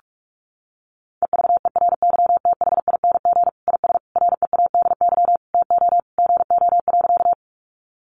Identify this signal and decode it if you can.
Morse code, heard as a keyed tone, about 44 wpm, 700 Hz. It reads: EVERYTHING IS READY TO GO.
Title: Sentences from Top 700 Words 45wpm